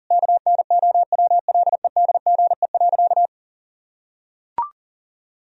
Morse code, heard as a keyed tone, about 40 wpm, 700 Hz. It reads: KNOWLEDGE.
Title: Sentences from Top 1000 Words - Review Part 3 - 40wpm